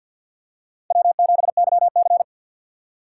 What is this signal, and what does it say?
Morse code, heard as a keyed tone, about 47 wpm, 700 Hz. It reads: W7YC
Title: USA Call Signs - 2x Character Spacing - 50wpm